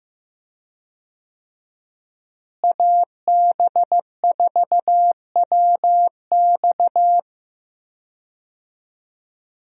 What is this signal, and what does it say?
Morse code, heard as a keyed tone, about 15 wpm, 700 Hz. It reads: AB4WX
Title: USA Call Signs - 5 Characters - Repeated 3x - 15wpm